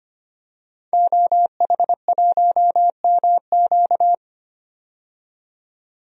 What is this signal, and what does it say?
Morse code, heard as a keyed tone, about 25 wpm, 700 Hz. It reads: OH1MQ
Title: DXCC-100 Call Signs - 5 Characters - 25wpm